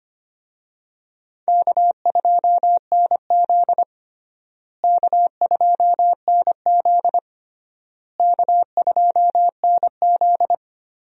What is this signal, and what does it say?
Morse code, heard as a keyed tone, about 25 wpm, 700 Hz. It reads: K2NZ K2NZ K2NZ
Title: USA Call Signs - 4 Characters - Repeated 3x - 25wpm